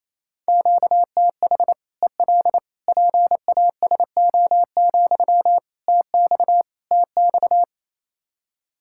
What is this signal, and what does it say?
Morse code, heard as a keyed tone, about 28 wpm, 700 Hz. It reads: QTH EL PASO, TX TX
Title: Practice QSO - Exchange 1 - Beginning - 28wpm